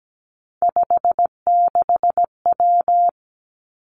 Morse code, heard as a keyed tone, about 17 wpm, 700 Hz. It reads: 56W
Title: Three Challenging Letter-Number Combinations 17wpm